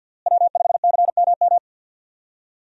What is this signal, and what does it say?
Morse code, heard as a keyed tone, about 51 wpm, 700 Hz. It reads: W5CMM
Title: USA Call Signs - 5 Characters - Repeated 2x - 50wpm